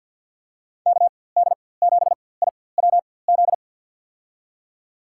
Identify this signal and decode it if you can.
Morse code, heard as a keyed tone, about 47 wpm, spaced as for 23 wpm, 700 Hz. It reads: KD7IWZ
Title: USA Call Signs - 6 Characters - 4x Character Spacing - 50wpm